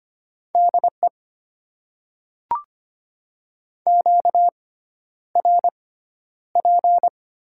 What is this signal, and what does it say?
Morse code, heard as a keyed tone, about 25 wpm, spaced as for 11 wpm, 700 Hz. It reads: DE QRP
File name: Top 90 QSO Elements - 6x Character Spacing - 25wpm